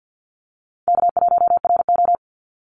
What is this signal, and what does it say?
Morse code, heard as a keyed tone, about 50 wpm, 700 Hz. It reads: K1RO
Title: Easy Call Signs - 50wpm